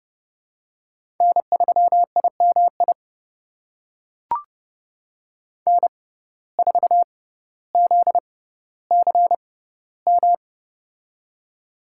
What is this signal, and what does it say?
Morse code, heard as a keyed tone, about 30 wpm, spaced as for 15 wpm, 700 Hz. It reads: N3IMI N4ZCM
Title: USA Call Signs - 5 Characters - 6x Character Spacing - 30wpm